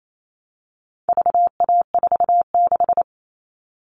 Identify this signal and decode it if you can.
Morse code, heard as a keyed tone, about 28 wpm, 700 Hz. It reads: VA46
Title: All Cut Numbers - 4-Digit 28wpm